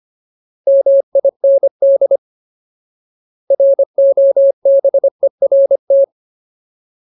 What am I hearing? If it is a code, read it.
Morse code, heard as a keyed tone, about 25 wpm, 550 Hz. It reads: MIND ROBERT
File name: Ultimate Word List / 25WPM Ultimate